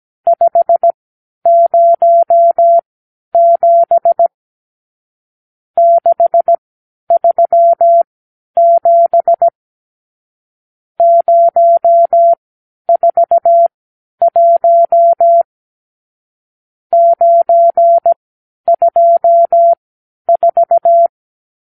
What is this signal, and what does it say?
Morse code, heard as a keyed tone, about 17 wpm, spaced as for 10 wpm, 680 Hz. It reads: 507 637 041 924
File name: arrl / Review (1-0)